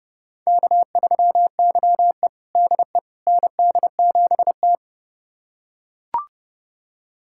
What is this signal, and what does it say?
Morse code, heard as a keyed tone, about 32 wpm, 700 Hz. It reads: K3YE DE ND7T
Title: USA Callsign DE Callsign - 4 Characters - 30wpm